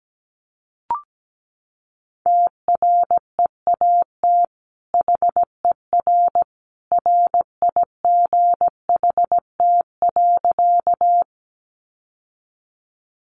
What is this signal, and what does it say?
Morse code, heard as a keyed tone, about 17 wpm, 700 Hz. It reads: TREAT HER RIGHT.
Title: Sentences from Top 1300 Words 17wpm